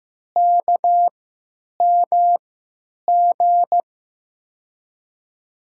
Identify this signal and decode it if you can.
Morse code, heard as a keyed tone, about 16 wpm, spaced as for 9 wpm, 700 Hz. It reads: KMG
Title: USA Call Signs - 6 Characters - 3x Character Spacing - 15wpm